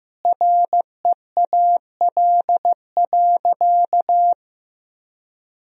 Morse code, heard as a keyed tone, about 15 wpm, 700 Hz. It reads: REAL.
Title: 2x Word-Spacing - Sentences from Top 300 Words - 15wpm